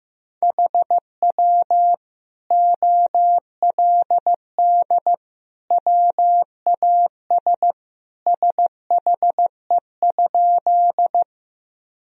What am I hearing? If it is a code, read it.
Morse code, heard as a keyed tone, about 15 wpm, 700 Hz. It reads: HW OLD WAS SHE?
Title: Sentences from Top 200 Words and Abbreviations 15wpm